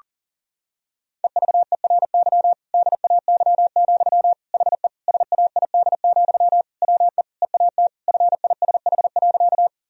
Morse code, heard as a keyed tone, about 40 wpm, 700 Hz. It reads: EVERY DAY, HE SAID, WE EAT FISH.
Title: Sentences from Top 1000 Words - Review Part 1 - 40wpm